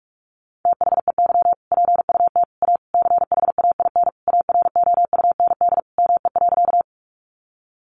Morse code, heard as a keyed tone, about 44 wpm, 700 Hz. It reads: THEY PUT A CHAIN AROUND ME.
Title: Sentences from Top 1500 Words 45wpm